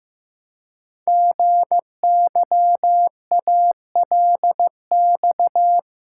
Morse code, heard as a keyed tone, about 15 wpm, 700 Hz. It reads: GYALX
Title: Code Groups - 5 Letters - 15wpm